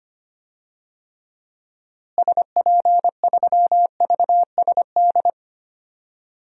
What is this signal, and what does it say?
Morse code, heard as a keyed tone, about 25 wpm, 700 Hz. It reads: SP3VSD
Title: DXCC-100 Call Signs - 6 Characters - Repeated 3x - 25wpm